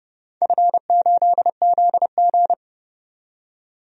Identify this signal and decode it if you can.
Morse code, heard as a keyed tone, about 31 wpm, 700 Hz. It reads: F8ZG
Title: DXCC-100 Call Signs - 4 Characters - 30wpm